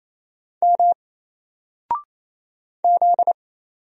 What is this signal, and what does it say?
Morse code, heard as a keyed tone, about 27 wpm, 700 Hz. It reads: M Z
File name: Single-Letter-Number-28wpm